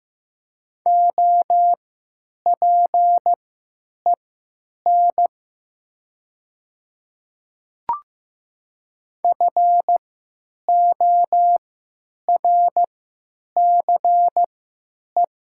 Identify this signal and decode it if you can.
Morse code, heard as a keyed tone, about 15 wpm, spaced as for 9 wpm, 700 Hz. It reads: OPEN FORCE
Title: 3x Character Spacing - Top 500 Words - 15wpm